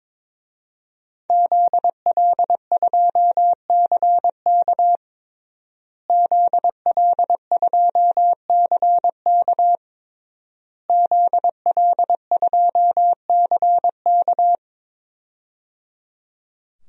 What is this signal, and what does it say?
Morse code, heard as a keyed tone, about 22 wpm, 700 Hz. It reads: ZL2CK ZL2CK ZL2CK
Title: DXCC-100 Call Signs - 5 Characters - Repeated 3x - 22wpm